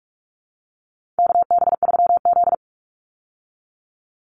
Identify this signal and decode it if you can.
Morse code, heard as a keyed tone, about 44 wpm, 700 Hz. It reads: KB3Z
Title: Call Signs Repeated 2x - 45wpm